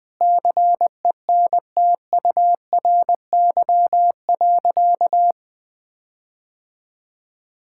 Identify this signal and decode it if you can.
Morse code, heard as a keyed tone, about 20 wpm, 700 Hz. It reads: CENTURY.
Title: Sentences with 5-Word Phrases - 20wpm